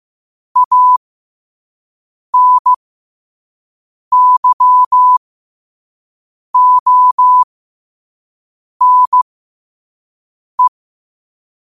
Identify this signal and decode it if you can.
Morse code, heard as a keyed tone, about 15 wpm, spaced as for 5 wpm, 1000 Hz. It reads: ANYONE
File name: qotdfiles / 5wpm